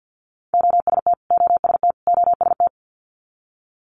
Morse code, heard as a keyed tone, about 50 wpm, 690 Hz. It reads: OST OST OST
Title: Multi-letter Phonemes Repeated 3x 50wpm